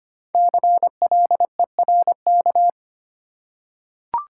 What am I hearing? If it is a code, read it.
Morse code, heard as a keyed tone, about 25 wpm, 700 Hz. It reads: CLERK
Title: Five Letter Words - 25wpm